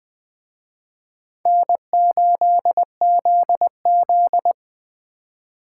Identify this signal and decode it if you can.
Morse code, heard as a keyed tone, about 21 wpm, 700 Hz. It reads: N8ZZ
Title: Speed Racing - USA 1x2 Call Signs 17wpm